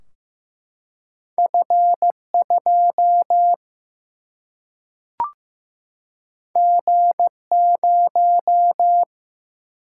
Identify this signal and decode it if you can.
Morse code, heard as a keyed tone, about 15 wpm, 700 Hz. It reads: F2 G0
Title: Single-Letter Single-Number 15wpm